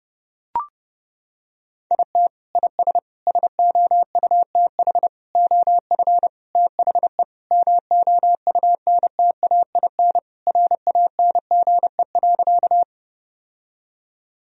Morse code, heard as a keyed tone, about 32 wpm, 700 Hz. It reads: IT IS SOUTH OF THE MOUNTAIN RANGE.
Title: Sentences from Top 1000 Words - Review Part 4 - 30wpm